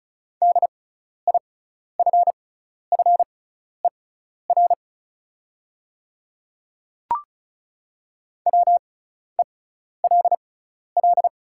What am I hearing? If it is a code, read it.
Morse code, heard as a keyed tone, about 34 wpm, spaced as for 12 wpm, 690 Hz. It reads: DIFFER WELL
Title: 6x Character Spacing - Top 200 Words - 35wpm